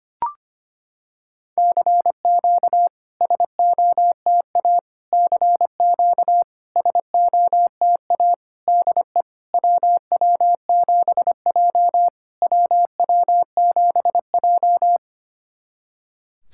Morse code, heard as a keyed tone, about 25 wpm, 690 Hz. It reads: CQ SOTA CQ SOTA DE WW7J WW7J
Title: SOTA QSO Practice 25wpm